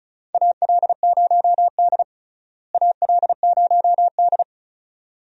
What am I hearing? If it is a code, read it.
Morse code, heard as a keyed tone, about 36 wpm, 700 Hz. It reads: AL0D AL0D
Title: USA Call Signs - 4 Characters - Repeated 2x - 35wpm